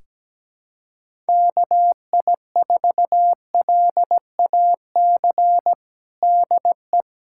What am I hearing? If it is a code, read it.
Morse code, heard as a keyed tone, about 17 wpm, 700 Hz. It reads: KI4LAC DE
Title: USA Callsign DE Callsign - 6 Characters - 17wpm